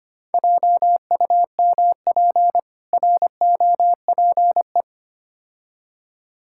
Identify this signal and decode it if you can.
Morse code, heard as a keyed tone, about 25 wpm, 690 Hz. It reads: JUMP ROPE
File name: practice / 25wpm / Difficult Words with Hints - 25wpm